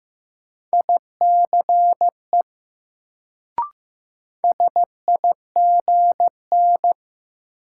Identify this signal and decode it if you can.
Morse code, heard as a keyed tone, about 15 wpm, 700 Hz. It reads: ICE SIGN
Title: Sentences from Top 1000 Words - New Word Drill - 15wpm